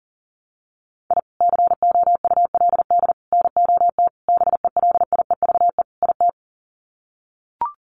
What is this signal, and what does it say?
Morse code, heard as a keyed tone, about 40 wpm, 700 Hz. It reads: I COULD NOT BELIEVE IT
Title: Sets of 5 Words Encore 40wpm